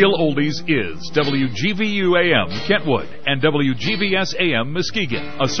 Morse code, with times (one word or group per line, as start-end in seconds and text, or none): none